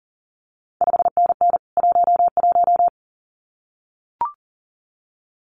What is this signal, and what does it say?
Morse code, heard as a keyed tone, about 39 wpm, 700 Hz. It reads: 5NN 11
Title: CQ WW DX Contest Exchange Encore - 40wpm